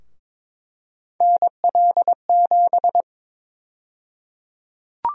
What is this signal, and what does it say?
Morse code, heard as a keyed tone, about 22 wpm, 700 Hz. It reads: NL7
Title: USA Call Sign Prefixes - Repeated 2x - 22wpm